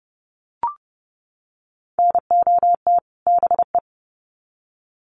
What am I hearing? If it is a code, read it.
Morse code, heard as a keyed tone, about 30 wpm, 700 Hz. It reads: NOT BE
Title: Sets of 2 Words 30wpm